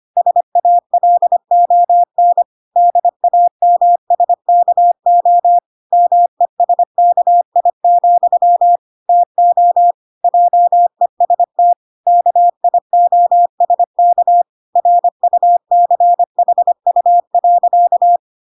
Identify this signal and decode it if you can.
Morse code, heard as a keyed tone, about 25 wpm, 700 Hz. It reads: SALON DAMSKO MESKI, TO JEST KIOSK RUCHU.